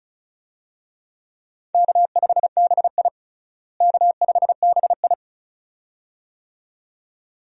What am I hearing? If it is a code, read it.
Morse code, heard as a keyed tone, about 36 wpm, 700 Hz. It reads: K5BI K5BI
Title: USA Call Signs - 4 Characters - Repeated 2x - 35wpm